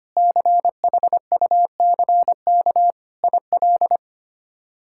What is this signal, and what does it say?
Morse code, heard as a keyed tone, about 25 wpm, 700 Hz. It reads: CHUCK IL
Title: K1USN SST QSO Practice - 25wpm